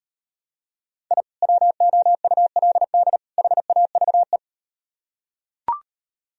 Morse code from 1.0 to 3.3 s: I WOULD

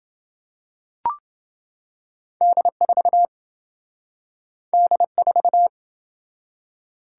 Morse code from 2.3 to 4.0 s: D4

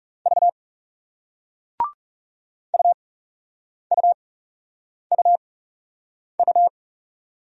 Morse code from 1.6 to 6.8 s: UUUU

